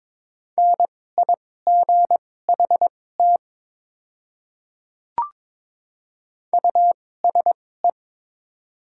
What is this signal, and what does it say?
Morse code, heard as a keyed tone, about 22 wpm, spaced as for 16 wpm, 700 Hz. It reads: NIGHT USE
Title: 2x Character Spacing - Top 300 Words - 22wpm